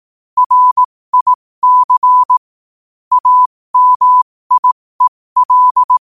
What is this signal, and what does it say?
Morse code, heard as a keyed tone, about 18 wpm, spaced as for 15 wpm, 1000 Hz. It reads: RIC AMIEL
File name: qotdfiles / 15wpm